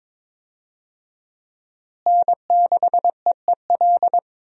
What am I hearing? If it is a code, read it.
Morse code, heard as a keyed tone, about 22 wpm, 700 Hz. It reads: N6EEL
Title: USA Call Signs - 5 Characters - Repeated 2x - 22wpm